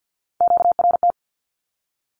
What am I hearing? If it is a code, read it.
Morse code, heard as a keyed tone, about 50 wpm, 700 Hz. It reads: QRT
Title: Q-Codes Repeated 3x 50wpm